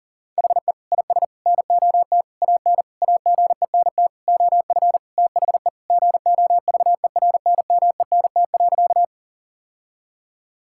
Morse code, heard as a keyed tone, about 40 wpm, 700 Hz. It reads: HE IS NOT AN AGENT OF THE GOVERNMENT.
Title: Sentences from Top 1600 Words 40wpm